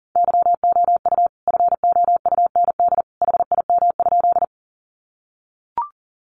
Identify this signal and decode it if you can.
Morse code, heard as a keyed tone, about 40 wpm, 700 Hz. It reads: YOU FOUND HIM?
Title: Sentences from Top 300 Words 40wpm